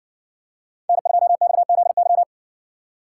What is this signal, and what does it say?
Morse code, heard as a keyed tone, about 56 wpm, 700 Hz. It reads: N2XZY